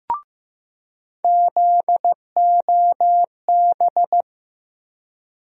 Morse code, heard as a keyed tone, about 15 wpm, 700 Hz. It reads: ZOB
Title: USA Call Sign Postfixes - 3 Letters 15wpm